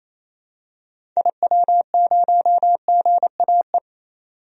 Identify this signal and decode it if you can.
Morse code, heard as a keyed tone, about 28 wpm, 700 Hz. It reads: IW0GAE
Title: DXCC-100 Call Signs - 6 Characters - 28wpm